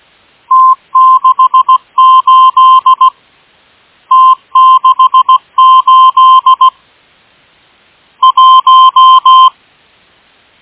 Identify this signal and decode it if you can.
Morse code, heard as a keyed tone, about 17 wpm, 1040 Hz. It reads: T68 T68 1
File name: M12- -1830utc-8157khz